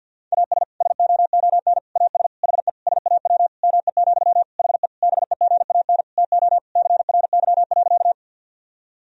Morse code, heard as a keyed tone, about 52 wpm, 700 Hz. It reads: AS SOON AS HE SAW ME, HE BEGAN TO CRY.
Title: Sentences from Top 1000 Words - Review Part 2 - 50wpm